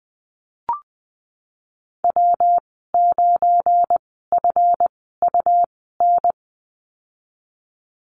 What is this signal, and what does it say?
Morse code, heard as a keyed tone, about 20 wpm, spaced as for 15 wpm, 700 Hz. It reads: W9FUN
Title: 179 Lesson 29g - Slash - Call Signs 20wpm